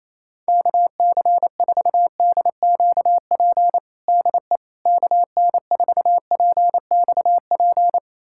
Morse code, heard as a keyed tone, about 28 wpm, 700 Hz. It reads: KC4DQP DE KN4PXP
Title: USA Callsign DE Callsign 28wpm